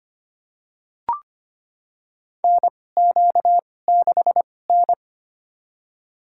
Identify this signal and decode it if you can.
Morse code, heard as a keyed tone, about 25 wpm, spaced as for 18 wpm, 700 Hz. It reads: NQ6N
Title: USA Call Signs - 4 Characters - 2x Character Spacing - 25wpm